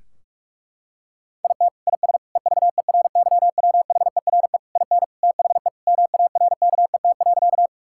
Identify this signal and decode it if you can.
Morse code, heard as a keyed tone, about 46 wpm, 700 Hz. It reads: IT IS EVERYWHERE IN THE MARKET.
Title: Sentences from Top 1600 Words 45wpm